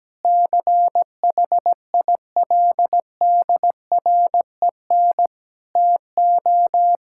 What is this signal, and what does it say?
Morse code, heard as a keyed tone, about 17 wpm, 700 Hz. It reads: CHILDREN TO
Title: Sentences from Top 1000 Words - Review Part 5 - 17wpm